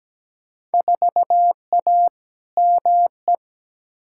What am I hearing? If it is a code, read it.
Morse code, heard as a keyed tone, about 17 wpm, 700 Hz. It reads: 4A ME
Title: Field Day Exchanges - 17wpm